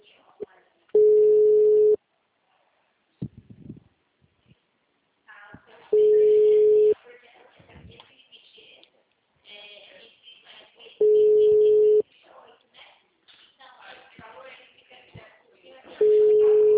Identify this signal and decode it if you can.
Ringback tone (Europe, 425 Hz), with speech.